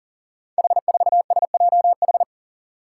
Morse code, heard as a keyed tone, about 40 wpm, 700 Hz. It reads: H4SJH
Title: Five Challenging Letter-Number Combinations 40wpm